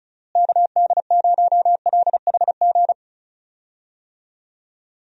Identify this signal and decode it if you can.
Morse code, heard as a keyed tone, about 34 wpm, 700 Hz. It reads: KD0LHG
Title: USA Call Signs - 6 Characters - Repeated 2x - 35wpm